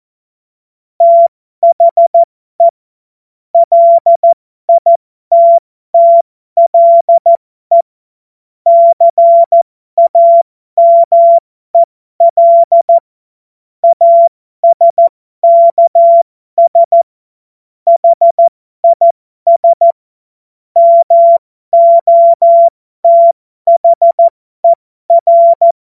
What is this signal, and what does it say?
Morse code, heard as a keyed tone, about 14 wpm, spaced as for 12 wpm, 680 Hz. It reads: THE LITTLE CAMEL ASKS HIS MOTHER
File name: Morsekurs / Witze